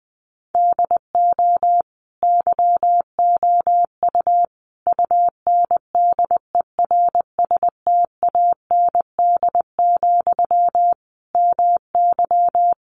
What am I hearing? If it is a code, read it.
Morse code, heard as a keyed tone, about 20 wpm, 700 Hz. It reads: DO YOU UNDERSTAND, MY
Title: Sentences from Top 800 Words 20wpm